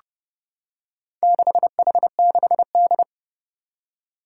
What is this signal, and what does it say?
Morse code, heard as a keyed tone, about 30 wpm, 700 Hz. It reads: 6H6D
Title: Four Challenging Letter-Number Combinations 30wpm